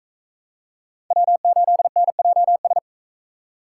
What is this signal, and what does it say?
Morse code, heard as a keyed tone, about 39 wpm, 700 Hz. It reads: W8NJS